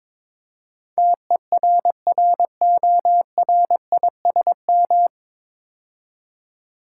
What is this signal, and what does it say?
Morse code, heard as a keyed tone, about 22 wpm, 700 Hz. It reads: TERRORISM
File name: 9 Letter Words - First 3 Letters Given - 22wpm